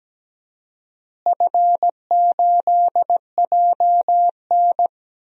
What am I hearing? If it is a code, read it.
Morse code, heard as a keyed tone, about 17 wpm, 700 Hz. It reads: F8JN
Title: DXCC-100 Call Signs - 4 Characters - Repeated 3x - 17wpm